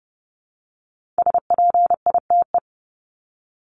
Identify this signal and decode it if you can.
Morse code, heard as a keyed tone, about 30 wpm, 700 Hz. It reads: SPITE
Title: Top 1751-2000 Words 30wpm